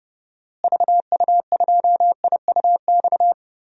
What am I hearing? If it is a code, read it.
Morse code, heard as a keyed tone, about 32 wpm, 700 Hz. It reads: VU2IUX
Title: DXCC-100 Call Signs - 6 Characters - Repeated 2x - 30wpm